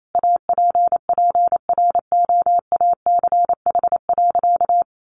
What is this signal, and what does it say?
Morse code, heard as a keyed tone, about 28 wpm, 700 Hz. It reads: APPROACH.
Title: Sentences from Top 1400 Words 28wpm